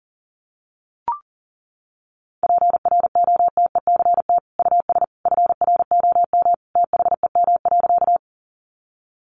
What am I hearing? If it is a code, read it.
Morse code, heard as a keyed tone, about 40 wpm, 700 Hz. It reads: PROTECT US FROM THEM.
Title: Sentences from Top 900 Words 40wpm